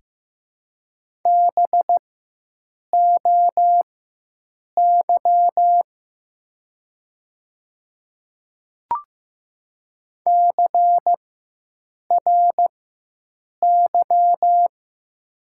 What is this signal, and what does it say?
Morse code, heard as a keyed tone, about 15 wpm, spaced as for 7 wpm, 700 Hz. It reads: BOY CRY
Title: 4x Character Spacing - Top 500 Words - 15wpm